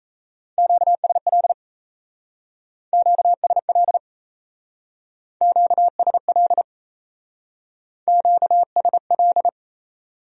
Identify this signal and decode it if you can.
Morse code, heard as a keyed tone, about 34 wpm, 700 Hz. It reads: QSL QSL QSL QSL
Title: Speed-Racing Q-Codes - 28wpm